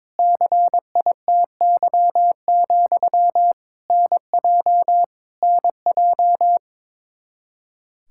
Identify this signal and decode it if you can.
Morse code, heard as a keyed tone, about 21 wpm, 700 Hz. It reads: CITY, NJ NJ
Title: Practice QSO - Exchange 1 - Beginning - 22wpm